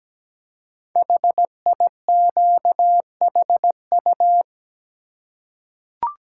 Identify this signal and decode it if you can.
Morse code, heard as a keyed tone, about 17 wpm, 700 Hz. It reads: HIQHU
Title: Code Groups - 5 Letters - 17wpm